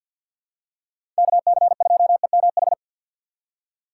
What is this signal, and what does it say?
Morse code, heard as a keyed tone, about 50 wpm, 700 Hz. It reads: KC2EMH
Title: USA Call Signs - 6 Characters - 3x Character Spacing - 50wpm